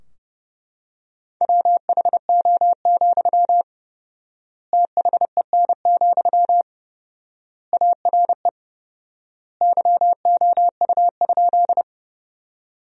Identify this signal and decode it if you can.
Morse code, heard as a keyed tone, about 30 wpm, 700 Hz. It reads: WHO, THEN, ARE YOU?
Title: 4x Word-Spacing - Sentences from Top 100 Words - 30wpm